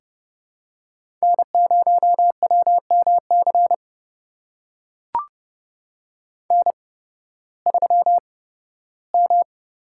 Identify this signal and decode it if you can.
Morse code, heard as a keyed tone, about 32 wpm, spaced as for 16 wpm, 700 Hz. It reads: N0WMC N3M
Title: USA Call Signs - 5 Characters - 8x Character Spacing - 30wpm